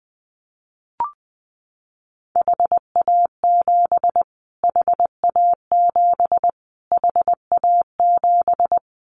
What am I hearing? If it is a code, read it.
Morse code, heard as a keyed tone, about 20 wpm, 700 Hz. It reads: HA7 HA7 HA7
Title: Callsign Prefixes for Top 25 DXCC - Repeated 3x - 20wpm